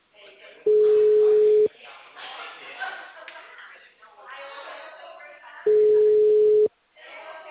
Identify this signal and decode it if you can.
Ringback tone (Europe, 425 Hz), with speech.
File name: out- -2002